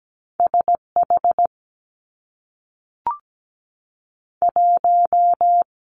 Morse code, heard as a keyed tone, about 17 wpm, 700 Hz. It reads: S H 1
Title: Two Challenging Letter-Number Combinations 17wpm